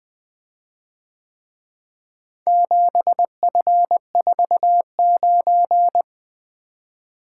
Morse code, heard as a keyed tone, about 22 wpm, 700 Hz. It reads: 7F49
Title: Speed-Racing - Four Letter-Number Combinations - 20wpm